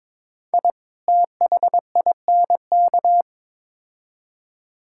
Morse code, heard as a keyed tone, about 22 wpm, 700 Hz. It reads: I THINK
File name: practice / 22wpm / Build A Phrase - 5-Word Phrases - 22wpm